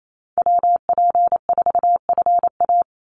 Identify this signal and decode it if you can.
Morse code, heard as a keyed tone, about 28 wpm, 700 Hz. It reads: WP4FA
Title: USA Call Signs 28wpm